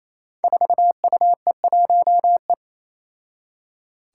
Morse code, heard as a keyed tone, about 28 wpm, 700 Hz. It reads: 4UE1E
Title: Five Challenging Letter-Number Combinations 28wpm